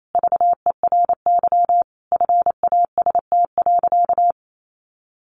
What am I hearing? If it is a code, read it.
Morse code, heard as a keyed tone, about 28 wpm, 700 Hz. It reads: VERY FAST.
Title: Sentences from Top 1100 Words 28wpm